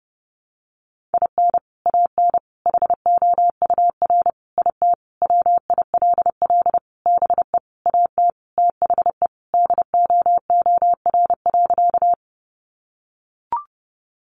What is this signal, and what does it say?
Morse code, heard as a keyed tone, about 30 wpm, 700 Hz. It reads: IN AN HOUR IT WILL BE AT THE DOOR.
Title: Sentences from Top 400 Words 30wpm